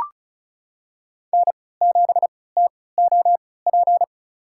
Morse code, heard as a keyed tone, about 36 wpm, spaced as for 20 wpm, 700 Hz. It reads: N7TOP
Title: USA Call Signs - 3x Character Spacing - 35wpm